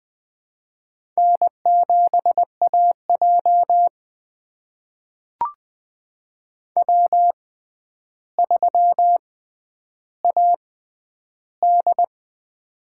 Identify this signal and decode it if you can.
Morse code, heard as a keyed tone, about 21 wpm, spaced as for 10 wpm, 700 Hz. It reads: N7AJ W3AD
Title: USA Call Signs - 6x Character Spacing - 20wpm